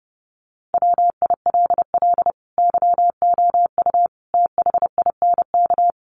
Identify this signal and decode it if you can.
Morse code, heard as a keyed tone, about 30 wpm, 700 Hz. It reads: WILL YOU THINK
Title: Sentences from Top 1200 Words 30wpm